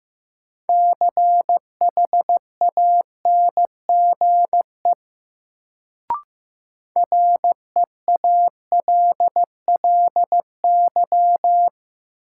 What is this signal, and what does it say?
Morse code, heard as a keyed tone, about 15 wpm, 700 Hz. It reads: CHANGE REALLY
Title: Sentences from Top 600 Words - New Word Drill - 15wpm